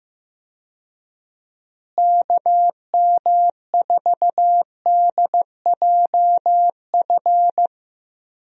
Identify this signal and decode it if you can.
Morse code, heard as a keyed tone, about 15 wpm, 700 Hz. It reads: KM4DJF
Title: USA Call Signs - 6 Characters - Repeated 2x - 15wpm